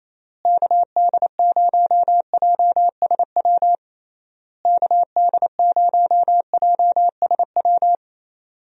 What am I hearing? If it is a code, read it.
Morse code, heard as a keyed tone, about 28 wpm, 700 Hz. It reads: KD0JSW KD0JSW
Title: USA Call Signs - 6 Characters - Repeated 3x - 28wpm